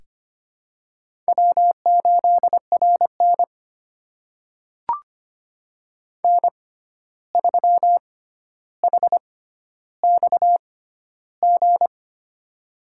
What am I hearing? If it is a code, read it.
Morse code, heard as a keyed tone, about 25 wpm, spaced as for 12 wpm, 700 Hz. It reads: W8RN N3HXG
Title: USA Call Signs - 6x Character Spacing - 25wpm